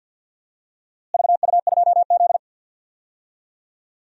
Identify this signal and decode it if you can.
Morse code, heard as a keyed tone, about 52 wpm, 700 Hz. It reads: VU2Z
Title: DXCC-100 Call Signs - 4 Characters - Repeated 3x - 50wpm